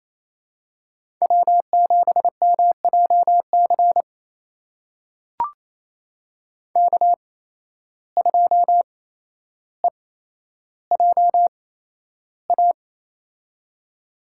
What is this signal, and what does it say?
Morse code, heard as a keyed tone, about 28 wpm, spaced as for 12 wpm, 700 Hz. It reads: W7MJC K2EJA
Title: USA Call Signs - 5 Characters - 8x Character Spacing - 28wpm